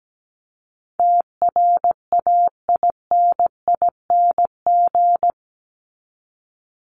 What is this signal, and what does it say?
Morse code, heard as a keyed tone, about 17 wpm, 700 Hz. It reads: TRAINING
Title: Top 501-750 Words 17wpm